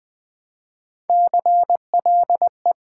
Morse code, heard as a keyed tone, about 20 wpm, 700 Hz. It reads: CLE
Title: Build A Word - 1 Letter At A Time - 5-Letter Words - 20wpm